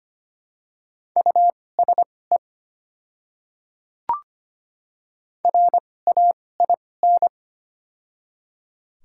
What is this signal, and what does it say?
Morse code, heard as a keyed tone, about 25 wpm, spaced as for 18 wpm, 700 Hz. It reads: USE RAIN
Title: 2x Character Spacing - Top 300 Words - 25wpm